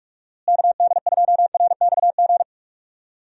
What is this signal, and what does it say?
Morse code, heard as a keyed tone, about 41 wpm, 700 Hz. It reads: KD2RXG
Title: USA Call Signs - 6 Characters - Repeated 2x - 45wpm